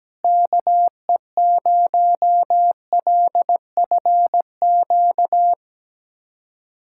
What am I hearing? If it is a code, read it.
Morse code, heard as a keyed tone, about 17 wpm, 700 Hz. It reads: KE0LFQ
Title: USA Callsign DE Callsign 17wpm